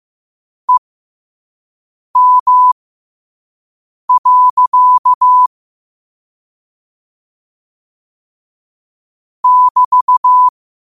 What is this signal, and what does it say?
Morse code, heard as a keyed tone, about 15 wpm, spaced as for 5 wpm, 1000 Hz. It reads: EM. <BT>
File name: 5wpm